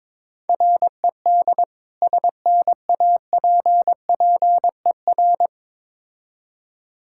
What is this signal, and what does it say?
Morse code, heard as a keyed tone, about 22 wpm, 700 Hz. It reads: RED SNAPPER
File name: Difficult Words with Hints - 22wpm